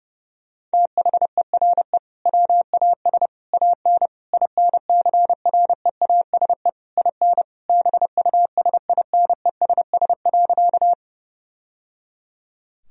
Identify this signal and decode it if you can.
Morse code, heard as a keyed tone, about 30 wpm, 700 Hz. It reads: THERE WAS AN INCREASE IN BUSINESS.
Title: Sentences from Top 1000 Words - Review Part 4 - 30wpm